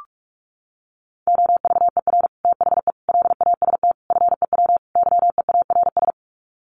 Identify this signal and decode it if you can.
Morse code, heard as a keyed tone, about 44 wpm, 700 Hz. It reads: OVER THE LAST FEW YEARS
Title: Sets of 5 Words Encore 45wpm